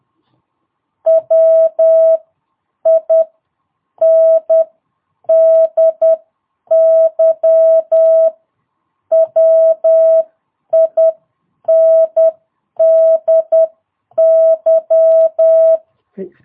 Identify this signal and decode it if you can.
Morse code, heard as a keyed tone, about 10 wpm, spaced as for 8 wpm, 660 Hz. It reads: WINDY WINDY
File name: jh8cbh / 2022cw